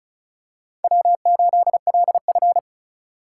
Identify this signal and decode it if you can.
Morse code, heard as a keyed tone, about 34 wpm, 700 Hz. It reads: W8LF
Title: USA Call Signs - 4 Characters - 8x Character Spacing - 35wpm